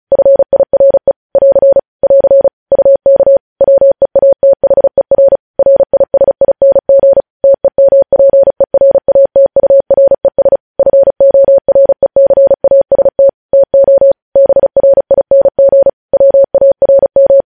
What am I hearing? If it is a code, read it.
Morse code, heard as a keyed tone, about 36 wpm, 550 Hz. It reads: FIRE <AR> <AR> UK WEATHER RISING TEMPERATURES FORECAST TO BRING WARM
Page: News Headlines in Morse Code at 35 WPM